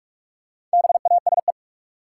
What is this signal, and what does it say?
Morse code, heard as a keyed tone, about 46 wpm, 700 Hz. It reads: BASE
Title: Four Letter Words - 45wpm